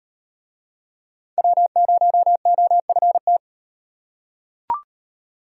Morse code, heard as a keyed tone, about 33 wpm, 700 Hz. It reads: W0OFT